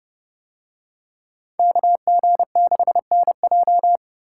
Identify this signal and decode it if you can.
Morse code, heard as a keyed tone, about 29 wpm, 700 Hz. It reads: KG6NJ